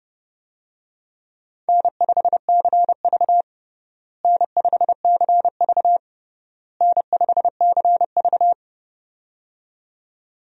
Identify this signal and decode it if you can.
Morse code, heard as a keyed tone, about 30 wpm, 700 Hz. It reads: N5CV N5CV N5CV
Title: USA Call Signs - 4 Characters - Repeated 3x - 30wpm